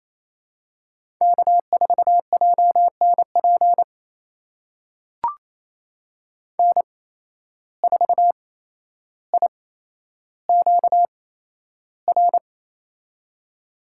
Morse code, heard as a keyed tone, about 28 wpm, spaced as for 12 wpm, 700 Hz. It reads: K4JNP N4IQR
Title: USA Call Signs - 8x Character Spacing - 28wpm